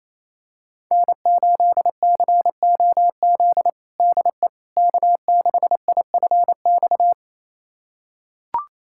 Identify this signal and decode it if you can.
Morse code, heard as a keyed tone, about 27 wpm, 700 Hz. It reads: N8COZ DE K6IFX
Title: USA Callsign DE Callsign - 5 Characters - 28wpm